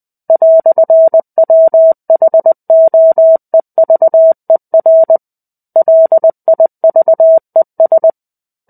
Morse code, heard as a keyed tone, about 20 wpm, 650 Hz. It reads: "WHOEVER LIVES
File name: 05-Chapter-05 20 WPM